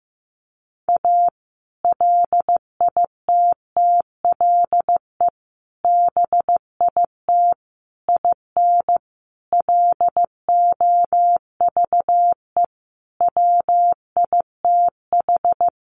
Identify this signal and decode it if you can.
Morse code, heard as a keyed tone, about 15 wpm, 700 Hz. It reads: A LITTLE BIT IN LOVE WITH
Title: Sentences from Top 900 Words 15wpm